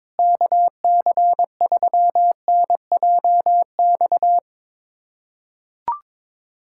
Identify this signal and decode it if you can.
Morse code, heard as a keyed tone, about 22 wpm, 700 Hz. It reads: KC3NJX
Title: USA Callsign DE Callsign - 6 Characters - 22wpm